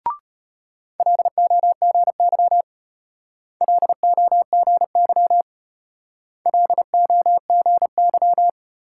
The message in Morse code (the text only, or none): LOGY LOGY LOGY